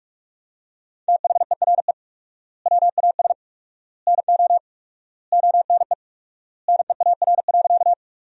Morse code, heard as a keyed tone, about 46 wpm, 700 Hz. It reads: THERE WAS NO ONE NEAR.
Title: 4x Word-Spacing - Sentences from Top 200 Words - 45wpm